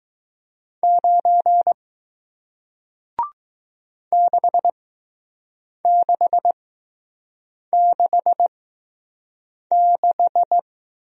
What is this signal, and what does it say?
Morse code, heard as a keyed tone, about 20 wpm, spaced as for 6 wpm, 700 Hz. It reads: 9 6666